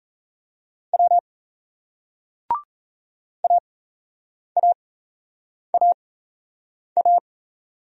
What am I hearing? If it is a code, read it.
Morse code, heard as a keyed tone, about 37 wpm, spaced as for 8 wpm, 700 Hz. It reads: W AAAA